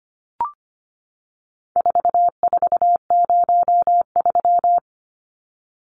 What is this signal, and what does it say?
Morse code, heard as a keyed tone, about 25 wpm, 700 Hz. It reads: 4403
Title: Numbers - 4-Digit 25wpm